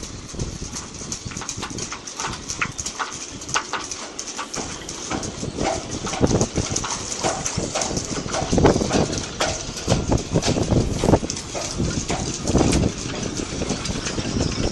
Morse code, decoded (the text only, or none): none